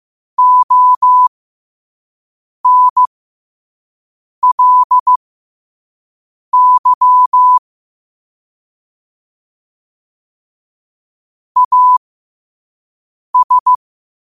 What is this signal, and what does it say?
Morse code, heard as a keyed tone, about 15 wpm, spaced as for 5 wpm, 1000 Hz. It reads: ONLY AS
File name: qotdfiles / 5wpm